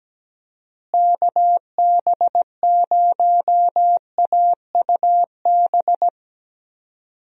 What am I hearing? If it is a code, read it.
Morse code, heard as a keyed tone, about 17 wpm, 700 Hz. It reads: KB0AUB
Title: USA Call Signs - 6 Characters - Repeated 3x - 17wpm